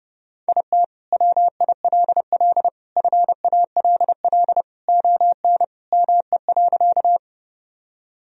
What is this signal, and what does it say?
Morse code, heard as a keyed tone, about 30 wpm, 700 Hz. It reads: IT WILL FALL ON ME.
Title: Sentences from Top 1000 Words - Review Part 2 - 30wpm